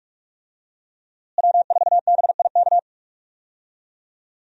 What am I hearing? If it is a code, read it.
Morse code, heard as a keyed tone, about 44 wpm, 700 Hz. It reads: W4BIK
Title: USA Call Signs - 5 Characters - Repeated 2x - 45wpm